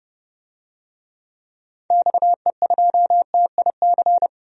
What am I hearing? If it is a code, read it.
Morse code, heard as a keyed tone, about 30 wpm, 700 Hz. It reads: XE2TIC
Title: DXCC-100 Call Signs - 6 Characters - 30wpm